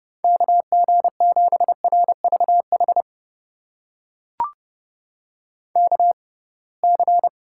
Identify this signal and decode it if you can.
Morse code, heard as a keyed tone, about 31 wpm, spaced as for 23 wpm, 700 Hz. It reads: KG7RVH KC
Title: USA Call Signs - 6 Characters - 6x Character Spacing - 30wpm